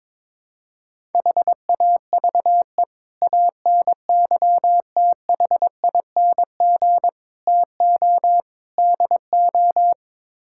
Morse code, heard as a keyed tone, about 22 wpm, 700 Hz. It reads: HAVE ANYTHING TO DO
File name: practice / 22wpm / Build A Phrase - 5-Word Phrases - 22wpm